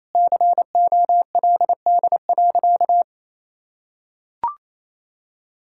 Morse code, heard as a keyed tone, about 28 wpm, 700 Hz. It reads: COLD.
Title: Sentences from Top 1000 Words - Review Part 5 - 28wpm